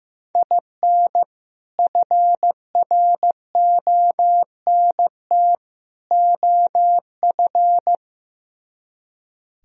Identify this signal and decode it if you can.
Morse code, heard as a keyed tone, about 16 wpm, 700 Hz. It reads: IN FRONT OF
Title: Speed-Racing - Sets of 3 Words 15wpm